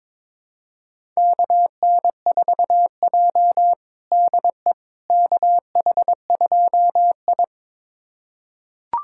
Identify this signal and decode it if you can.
Morse code, heard as a keyed tone, about 22 wpm, 700 Hz. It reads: KN4J DE KH2I
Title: USA Callsign DE Callsign - 4 Characters - 22wpm